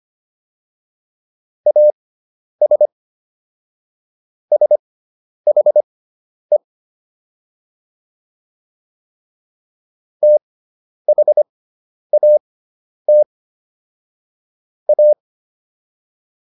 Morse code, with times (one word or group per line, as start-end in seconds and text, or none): 1.7-2.9 AS
4.5-6.6 SHE
10.2-13.2 THAT
14.9-15.1 A